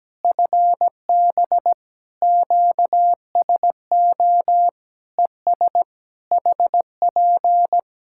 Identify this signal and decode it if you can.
Morse code, heard as a keyed tone, about 17 wpm, 700 Hz. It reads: FB QSO ES HP
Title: Practice QSO - Exchange 4 - Ending - 17wpm